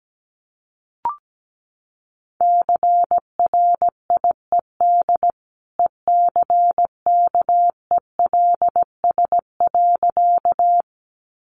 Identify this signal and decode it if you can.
Morse code, heard as a keyed tone, about 17 wpm, 700 Hz. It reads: CRIED ECKELS.
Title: A Sound Of Thunder 17wpm